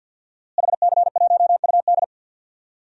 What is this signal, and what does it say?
Morse code, heard as a keyed tone, about 52 wpm, 700 Hz. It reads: HC1UD
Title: DXCC-100 Call Signs - 5 Characters - Repeated 3x - 50wpm